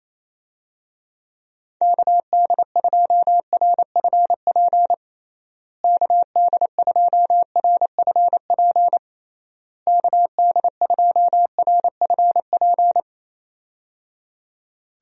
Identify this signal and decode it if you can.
Morse code, heard as a keyed tone, about 27 wpm, 700 Hz. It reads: KD2RFP KD2RFP KD2RFP
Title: USA Call Signs - 6 Characters - Repeated 3x - 28wpm